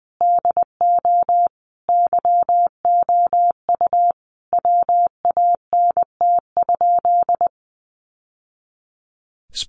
Morse code, heard as a keyed tone, about 20 wpm, 700 Hz. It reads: DO YOU WANT?
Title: Sentences from Top 1100 Words 20wpm